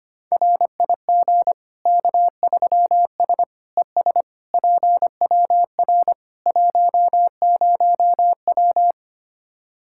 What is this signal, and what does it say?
Morse code, heard as a keyed tone, about 25 wpm, 700 Hz. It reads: RIG K3S ES PWR 10W
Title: Practice QSO - Exchange 2 - Info - 25wpm